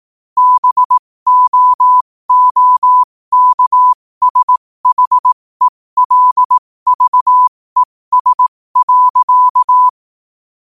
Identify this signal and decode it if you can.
Morse code, heard as a keyed tone, about 18 wpm, spaced as for 15 wpm, 1000 Hz. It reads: BOOKSHELVES.
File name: qotdfiles / 15wpm